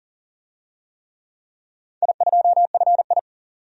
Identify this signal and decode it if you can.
Morse code, heard as a keyed tone, about 40 wpm, 700 Hz. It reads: I2FI
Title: DXCC-100 Call Signs - 4 Characters - 40wpm